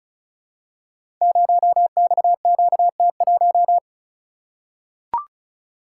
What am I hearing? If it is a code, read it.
Morse code, heard as a keyed tone, about 36 wpm, 700 Hz. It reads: 0XQT1
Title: Code Groups - 5 Letter:Numbers - 35wpm